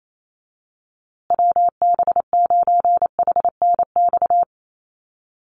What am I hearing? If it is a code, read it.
Morse code, heard as a keyed tone, about 28 wpm, 700 Hz. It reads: WB9HNX
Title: USA Call Signs Repeated 3x 28wpm